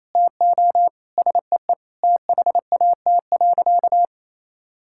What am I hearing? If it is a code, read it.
Morse code, heard as a keyed tone, about 27 wpm, 700 Hz. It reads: TO SEE THAT.
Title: Sentences from Top 1600 Words 28wpm